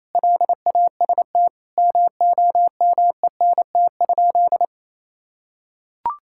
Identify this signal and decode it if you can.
Morse code, heard as a keyed tone, about 28 wpm, 700 Hz. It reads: LAST MOMENT?
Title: Sentences from Top 1600 Words 28wpm